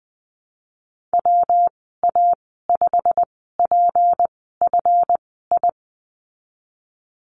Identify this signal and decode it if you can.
Morse code, heard as a keyed tone, about 19 wpm, spaced as for 14 wpm, 700 Hz. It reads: WA5PFI
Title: 173 Lesson 28g - Number 9 - Call Signs 20wpm